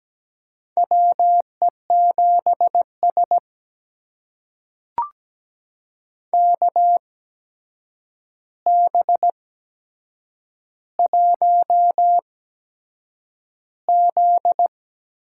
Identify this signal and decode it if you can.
Morse code, heard as a keyed tone, about 17 wpm, spaced as for 7 wpm, 700 Hz. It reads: WE7S KB1Z
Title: USA Call Signs - 4 Characters - 8x Character Spacing - 17wpm